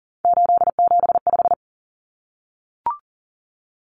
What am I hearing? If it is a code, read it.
Morse code, heard as a keyed tone, about 40 wpm, 700 Hz. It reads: 875
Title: Numbers - 3-Digit 40wpm